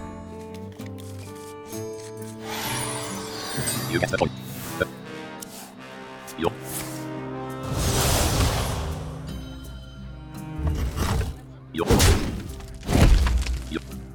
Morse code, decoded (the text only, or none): none